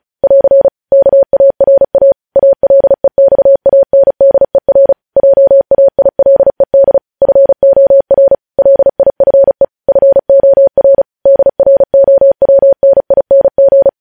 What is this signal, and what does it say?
Morse code, heard as a keyed tone, about 36 wpm, 550 Hz. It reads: <AR> KARA ALEXANDER JAILED FOR LIFE FOR DROWNING
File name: podcast / News headlines-1x-35x35